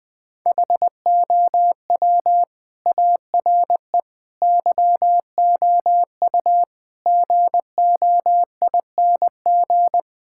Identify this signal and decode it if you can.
Morse code, heard as a keyed tone, about 20 wpm, 700 Hz. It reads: HOW ARE YOU GOING
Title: Sentences with 5-Word Phrases - 20wpm